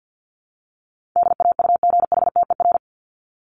Morse code, heard as a keyed tone, about 49 wpm, 700 Hz. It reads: DAUGHTER
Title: Top 1251-1500 Words 50wpm